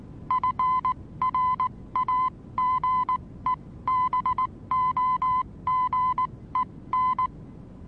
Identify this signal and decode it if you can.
Morse code, heard as a keyed tone, about 19 wpm, spaced as for 16 wpm, 1020 Hz. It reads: FRAGEBOGEN
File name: repos / Morsekurs / DJ4UF Morsen26